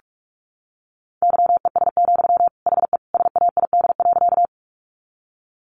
Morse code, heard as a keyed tone, about 44 wpm, 700 Hz. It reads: YES, HE SAID.
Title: Sentences from Top 500 Words 45wpm